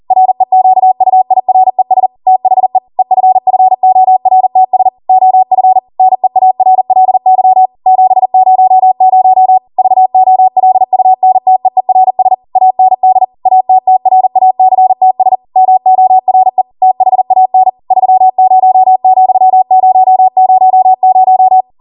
Morse code, heard as a keyed tone, about 40 wpm, 750 Hz. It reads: REQUIRES THE EFFORTS OF NEARLY 700 VOLUNTEERS AND ATTRACTS MORE THAN 30,000